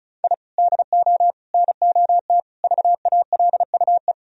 Morse code, heard as a keyed tone, about 36 wpm, 700 Hz. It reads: I DO NOT VALUE
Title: Sentences from Top 1000 Words - Review Part 4 - 35wpm